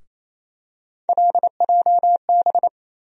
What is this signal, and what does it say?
Morse code, heard as a keyed tone, about 28 wpm, 700 Hz. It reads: LJB
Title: USA Call Sign Postfixes - 3 Letters 28wpm